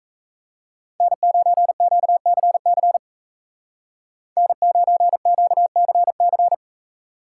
Morse code, heard as a keyed tone, about 42 wpm, 700 Hz. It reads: N9QCC N9QCC